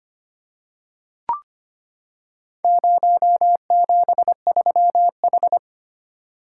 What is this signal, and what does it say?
Morse code, heard as a keyed tone, about 25 wpm, 700 Hz. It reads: 073H
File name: practice / 25wpm / Four Letter-Number Combinations - 25wpm